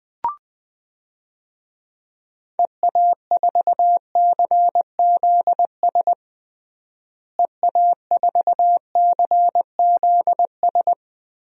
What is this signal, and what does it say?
Morse code, heard as a keyed tone, about 20 wpm, 700 Hz. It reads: EA4CZS EA4CZS
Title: DXCC-100 Call Signs - 6 Characters - Repeated 3x - 20wpm